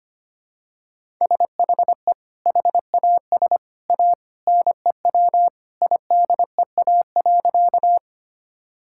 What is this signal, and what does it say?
Morse code, heard as a keyed tone, about 25 wpm, 700 Hz. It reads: SHE HAS A NEW IDEA.
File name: Sentences with 2-Word Phrases - 25wpm